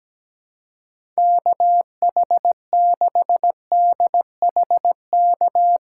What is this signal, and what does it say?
Morse code, heard as a keyed tone, about 17 wpm, 700 Hz. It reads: KH6DHK
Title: K1USN SST QSO Practice - 17wpm